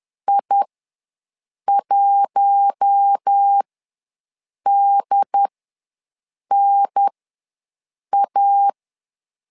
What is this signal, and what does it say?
Morse code, heard as a keyed tone, about 10 wpm, spaced as for 6 wpm, 800 Hz. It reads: I1DNA
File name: LEZIONI / 4 Q1 ws 10wpm 2538